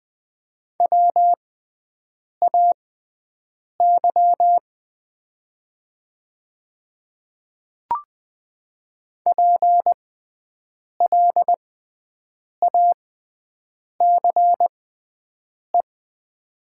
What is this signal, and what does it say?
Morse code, heard as a keyed tone, about 20 wpm, spaced as for 7 wpm, 700 Hz. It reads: WAY PLACE
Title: 6x Character Spacing - Top 500 Words - 20wpm